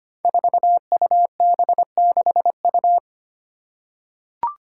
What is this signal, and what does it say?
Morse code, heard as a keyed tone, about 25 wpm, 700 Hz. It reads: 4UB6U
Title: Five Challenging Letter-Number Combinations 25wpm